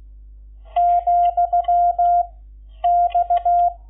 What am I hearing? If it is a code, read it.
Morse code, heard as a keyed tone, about 16 wpm, 700 Hz. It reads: , X
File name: Lecciones / CLASE Nº11